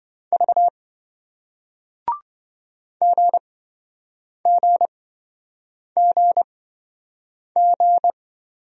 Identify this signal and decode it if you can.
Morse code, heard as a keyed tone, about 25 wpm, spaced as for 7 wpm, 700 Hz. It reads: V GGGG
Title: Speed-Racing Single Letters 20wpm